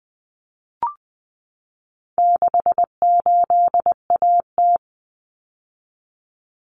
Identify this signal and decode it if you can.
Morse code, heard as a keyed tone, about 20 wpm, 700 Hz. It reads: 68AT
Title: Common Cut Numbers - 4-Digit 20wpm